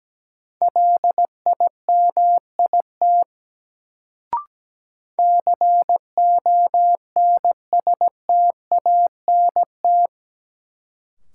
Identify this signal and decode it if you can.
Morse code, heard as a keyed tone, about 17 wpm, 700 Hz. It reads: LIMIT CONSTANT
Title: Sentences from Top 1500 Words - New Word Drill Alt - 17wpm